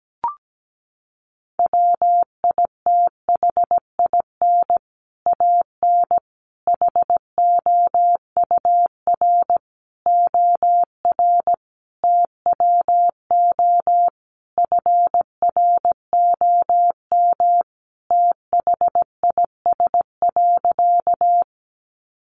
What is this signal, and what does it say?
Morse code, heard as a keyed tone, about 17 wpm, 700 Hz. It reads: WITHIN AN HOUR OR TWO FROM THIS.
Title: Sentences from Top 700 Words 17wpm